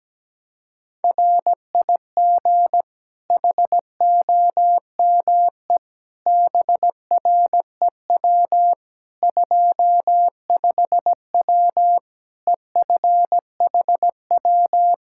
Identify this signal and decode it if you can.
Morse code, heard as a keyed tone, about 17 wpm, 700 Hz. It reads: RIG HOME BREW 25W EFHW
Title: K1USN SST QSO Practice 17wpm